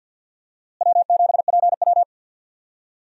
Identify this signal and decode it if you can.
Morse code, heard as a keyed tone, about 45 wpm, 700 Hz. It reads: W7PW
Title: USA Call Signs - 4 Characters - Repeated 3x - 50wpm